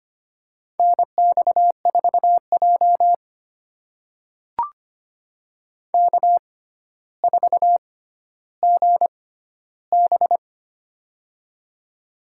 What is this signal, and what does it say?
Morse code, heard as a keyed tone, about 25 wpm, spaced as for 13 wpm, 700 Hz. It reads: NX4J K4GB
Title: USA Call Signs - 4 Characters - 6x Character Spacing - 25wpm